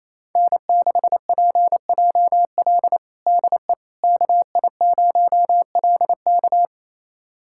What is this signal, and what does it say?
Morse code, heard as a keyed tone, about 27 wpm, 700 Hz. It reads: N6PJL DE KI0LK
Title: USA Callsign DE Callsign - 5 Characters - 28wpm